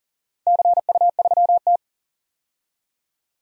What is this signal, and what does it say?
Morse code, heard as a keyed tone, about 40 wpm, 700 Hz. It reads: CU3T
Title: DXCC-100 Call Signs - 4 Characters - 40wpm